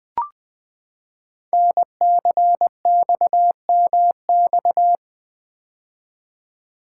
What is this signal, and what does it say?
Morse code, heard as a keyed tone, about 20 wpm, 700 Hz. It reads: NCXMX
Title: Code Groups - 5 Letters - 20wpm